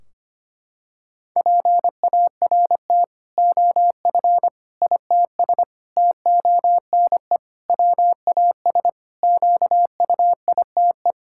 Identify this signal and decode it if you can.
Morse code, heard as a keyed tone, about 25 wpm, 700 Hz. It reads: PART OF ITS TONE WAS QUITE
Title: Sentences from Top 1600 Words 25wpm